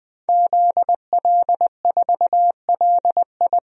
Morse code, heard as a keyed tone, about 20 wpm, 700 Hz. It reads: ZL4LI
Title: DXCC-100 Call Signs - 5 Characters - 20wpm